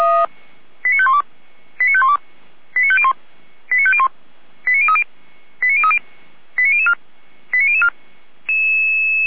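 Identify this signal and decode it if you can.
Five-tone selective call (ZVEI), heard as quick runs of five tones, tones of about 70 ms. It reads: 89512 89512 89501 89501 89030 89030 89004 89004
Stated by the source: ZVEI five-tone call received by SDR.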